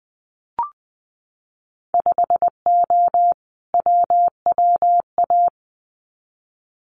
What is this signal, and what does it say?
Morse code, heard as a keyed tone, about 20 wpm, 700 Hz. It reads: 5O WWA
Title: Winter Field Day Exchanges Encore 20wpm